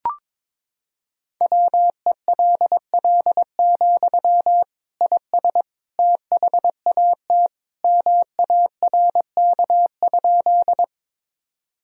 Morse code, heard as a keyed tone, about 21 wpm, 700 Hz. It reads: WELL, IS THAT MARK?
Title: Sentences from Top 1000 Words - Review Part 1 - 22wpm